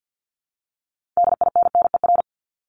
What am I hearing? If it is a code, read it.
Morse code, heard as a keyed tone, about 50 wpm, 700 Hz. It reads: DINNER
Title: Top 1751-2000 Words 50wpm